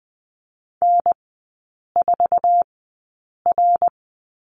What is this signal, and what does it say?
Morse code, heard as a keyed tone, about 20 wpm, spaced as for 8 wpm, 700 Hz. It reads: N4R
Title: 053 Lesson 10e - Letter r - Three Characters 20wpm